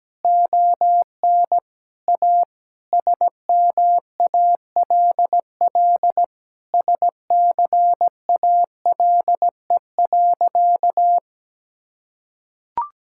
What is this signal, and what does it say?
Morse code, heard as a keyed tone, about 17 wpm, 700 Hz. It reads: ON A SMALL SCALE.
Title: Sentences from Top 1000 Words - Review Part 5 - 17wpm